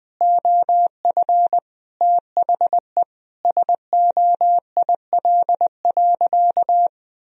Morse code, heard as a keyed tone, about 20 wpm, 700 Hz. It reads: OF THE SOIL.
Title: Sentences from Top 1000 Words - Review Part 4 - 20wpm